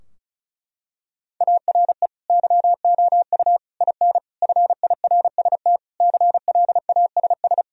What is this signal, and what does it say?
Morse code, heard as a keyed tone, about 36 wpm, 700 Hz. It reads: ARE YOU IN FIRST CLASS
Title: Sentences from Top 1000 Words - Review Part 2 - 35wpm